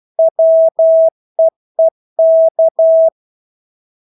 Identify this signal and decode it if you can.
Morse code, heard as a keyed tone, about 12 wpm, 650 Hz. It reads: WEEK
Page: Engelske ord 12 wpm